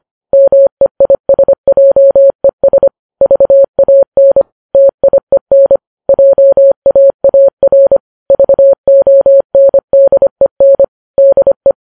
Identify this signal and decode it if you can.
Morse code, heard as a keyed tone, about 25 wpm, 550 Hz. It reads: MEISJES VAN TIEN JAAR VONDEN DE